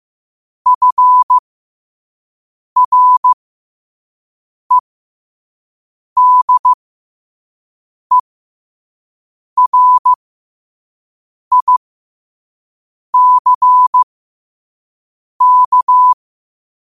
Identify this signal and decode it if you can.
Morse code, heard as a keyed tone, about 15 wpm, spaced as for 5 wpm, 1000 Hz. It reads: FREDERICK
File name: qotdfiles / 5wpm